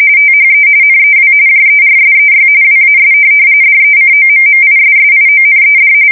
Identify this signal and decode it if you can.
RTTY, about 45.45 baud. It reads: THE NEW JERSEY WAS DECOMMISSIONED
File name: fsk170 / USS-New-Jersey